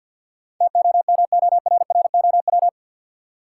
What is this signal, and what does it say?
Morse code, heard as a keyed tone, about 53 wpm, 700 Hz. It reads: TOMORROW
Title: Sentences from Top 1100 Words - New Word Drill - 50wpm